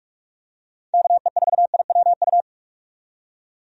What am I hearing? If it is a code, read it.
Morse code, heard as a keyed tone, about 43 wpm, 700 Hz. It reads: KE4IWU
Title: USA Call Signs - 6 Characters - 45wpm